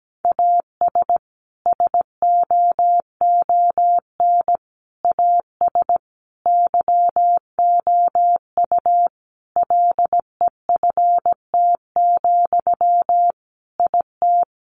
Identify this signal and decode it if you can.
Morse code, heard as a keyed tone, about 17 wpm, 700 Hz. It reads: AS SOON AS YOU LEFT, IT
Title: Sentences from Top 400 Words 17wpm